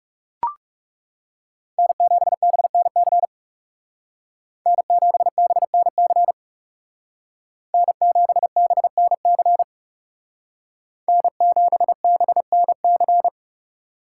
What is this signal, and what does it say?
Morse code, heard as a keyed tone, about 37 wpm, 700 Hz. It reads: N7BNC N7BNC N7BNC N7BNC
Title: Speed Racing - USA Call Signs 30wpm